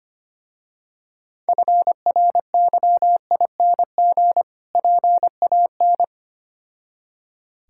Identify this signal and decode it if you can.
Morse code, heard as a keyed tone, about 25 wpm, 700 Hz. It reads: FRYING PAN
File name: practice / 25wpm / Difficult Words with Hints - 25wpm